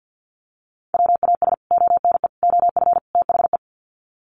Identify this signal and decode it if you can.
Morse code, heard as a keyed tone, about 50 wpm, 700 Hz. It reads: WAS ONE OF THE
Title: Sets of 4 Words 50wpm